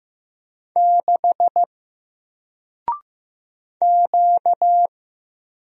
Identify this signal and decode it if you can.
Morse code, heard as a keyed tone, about 15 wpm, 700 Hz. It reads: 6 Q
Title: Single Letter-Number Alt 15wpm